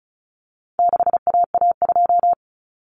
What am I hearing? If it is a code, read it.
Morse code, heard as a keyed tone, about 34 wpm, 700 Hz. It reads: 6AA2
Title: Common Cut Numbers - 4-Digit 35wpm